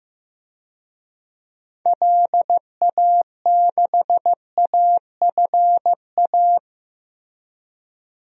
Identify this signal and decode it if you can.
Morse code, heard as a keyed tone, about 15 wpm, 700 Hz. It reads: LA6AFA
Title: DXCC-100 Call Signs - 6 Characters - Repeated 2x - 15wpm